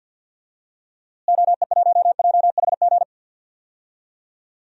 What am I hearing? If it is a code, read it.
Morse code, heard as a keyed tone, about 52 wpm, 700 Hz. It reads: OE1JHG
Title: DXCC-100 Call Signs - 6 Characters - Repeated 3x - 50wpm